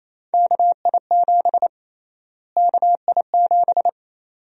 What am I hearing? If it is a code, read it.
Morse code, heard as a keyed tone, about 28 wpm, 700 Hz. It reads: KI7 KI7
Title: USA Call Sign Prefixes - Repeated 2x - 28wpm